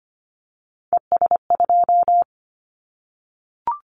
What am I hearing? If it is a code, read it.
Morse code, heard as a keyed tone, about 25 wpm, 700 Hz. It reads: ES2
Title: Callsign Prefixes for Top 50 DXCC - 25wpm